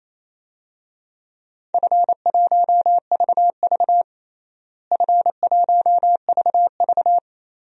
Morse code, heard as a keyed tone, about 27 wpm, 700 Hz. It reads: F1VV F1VV
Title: DXCC-100 Call Signs - 4 Characters - Repeated 3x - 28wpm